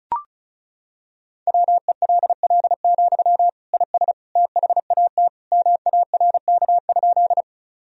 Morse code, heard as a keyed tone, about 36 wpm, 700 Hz. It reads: WELL, IS THAT MARK?
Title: Sentences from Top 1000 Words - Review Part 1 - 35wpm